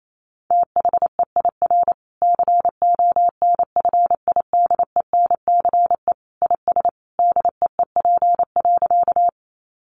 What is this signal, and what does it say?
Morse code, heard as a keyed tone, about 28 wpm, 700 Hz. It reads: THEIR CONFIDENCE IS DEEP.
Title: Sentences from Top 1500 Words 28wpm